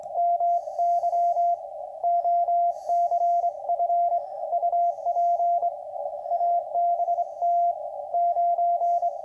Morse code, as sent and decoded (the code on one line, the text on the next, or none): -- -.-- / --- -.-. ..-. / ..- .--. / .- -... - / ----.
MY OCF UP ABT 9